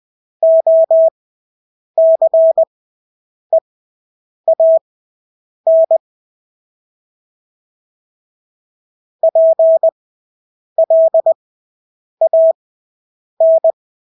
Morse code, heard as a keyed tone, about 20 wpm, spaced as for 8 wpm, 650 Hz. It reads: OCEAN PLAN
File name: Engelske ord 20/8 wpm